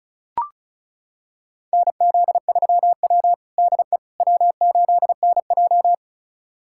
Morse code, heard as a keyed tone, about 34 wpm, 700 Hz. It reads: NZ3W DE W8NJ
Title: USA Callsign DE Callsign - 4 Characters - 35wpm